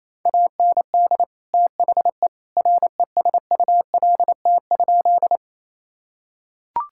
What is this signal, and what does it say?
Morse code, heard as a keyed tone, about 28 wpm, 700 Hz. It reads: AND THE RESULT?
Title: Sentences from Top 1000 Words - Review Part 5 - 28wpm